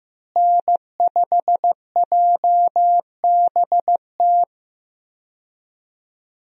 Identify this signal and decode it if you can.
Morse code, heard as a keyed tone, about 16 wpm, 700 Hz. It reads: N5JBT
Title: USA Call Signs - 5 Characters - Repeated 3x - 15wpm